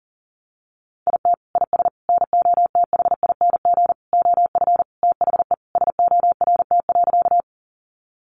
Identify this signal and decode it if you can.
Morse code, heard as a keyed tone, about 40 wpm, 700 Hz. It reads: IT IS NOTHING OF THE SORT.
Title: Sentences from Top 1300 Words 40wpm